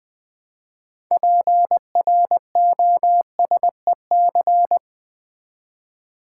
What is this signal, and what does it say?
Morse code, heard as a keyed tone, about 20 wpm, 700 Hz. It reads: PROSEC
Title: Build A Word - 3 Letters At A Time - 9-Letter Words - 20wpm